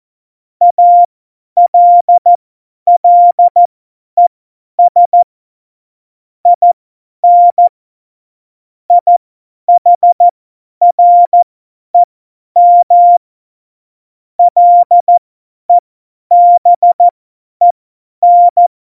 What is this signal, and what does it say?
Morse code, heard as a keyed tone, about 14 wpm, spaced as for 10 wpm, 700 Hz. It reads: ALLES IN IHREM LEBEN
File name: Morsekurs / Witze / Witz 03.Dezember 2019, 50 F70